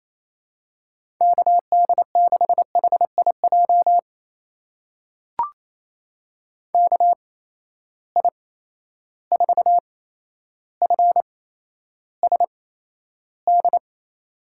Morse code, heard as a keyed tone, about 28 wpm, spaced as for 12 wpm, 700 Hz. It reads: KD6HIJ KI4FSD
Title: USA Call Signs - 8x Character Spacing - 28wpm